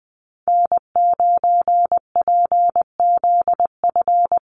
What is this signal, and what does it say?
Morse code, heard as a keyed tone, about 20 wpm, 700 Hz. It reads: N9PZF
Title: USA Call Signs Repeated 3x 20wpm